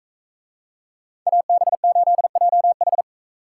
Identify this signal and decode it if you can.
Morse code, heard as a keyed tone, about 40 wpm, 700 Hz. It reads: AB8JH